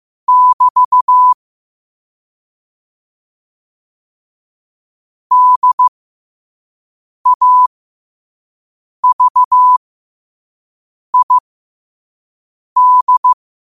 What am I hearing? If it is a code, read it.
Morse code, heard as a keyed tone, about 15 wpm, spaced as for 5 wpm, 1000 Hz. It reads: <BT> DAVID
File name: qotdfiles / 5wpm